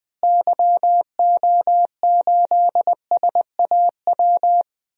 Morse code, heard as a keyed tone, about 21 wpm, 700 Hz. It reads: YO8SAW
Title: DXCC-100 Call Signs - 6 Characters - Repeated 3x - 20wpm